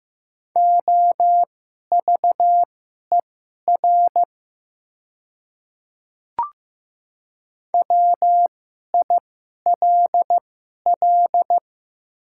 Morse code, heard as a keyed tone, about 15 wpm, spaced as for 11 wpm, 700 Hz. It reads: OVER WILL
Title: 2x Character Spacing - Top 100 Words - 15wpm